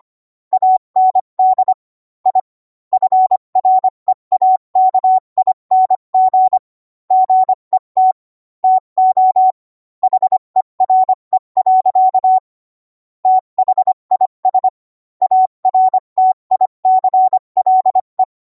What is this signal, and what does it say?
Morse code, heard as a keyed tone, about 25 wpm, 750 Hz. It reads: AND I FREAKING GET TO HERE. THIS ARTICLE